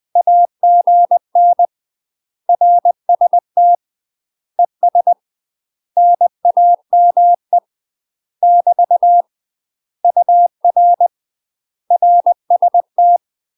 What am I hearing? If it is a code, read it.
Morse code, heard as a keyed tone, about 20 wpm, 700 Hz. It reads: AGN RST ES NAME <BT> UR RST
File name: UK MASTER QSOs TRAINING / 20WPM / UK MASTER QSOs 0040